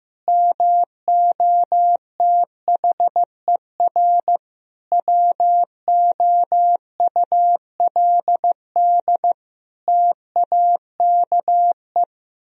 Morse code, heard as a keyed tone, about 16 wpm, 700 Hz. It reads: MOTHER WOULD TAKE
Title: Sentences from Top 1000 Words - Review Part 4 - 15wpm